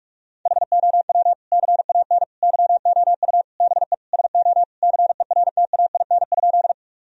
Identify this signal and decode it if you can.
Morse code, heard as a keyed tone, about 41 wpm, 700 Hz. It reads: HOW CAN YOU BE SO CERTAIN?
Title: Sentences from Top 1000 Words - Review Part 2 - 45wpm